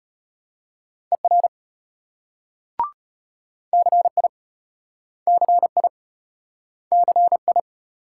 Morse code, heard as a keyed tone, about 36 wpm, 700 Hz. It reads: ER CI CI CI